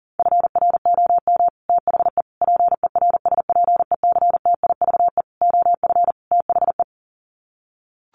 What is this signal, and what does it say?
Morse code, heard as a keyed tone, about 40 wpm, 700 Hz. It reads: FROM THE PERSPECTIVE OF THE
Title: Sets of 5 Words Encore2 40wpm